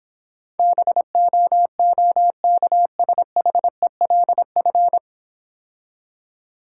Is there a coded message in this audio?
Yes: Morse code from 0.6 to 5.0 s, heard as a keyed tone, about 26 wpm, 690 Hz. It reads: BOOKSHELF